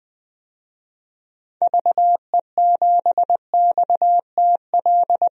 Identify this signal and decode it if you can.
Morse code, heard as a keyed tone, about 20 wpm, 700 Hz. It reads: VE7XTL
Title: DXCC-100 Call Signs - 6 Characters - Repeated 3x - 20wpm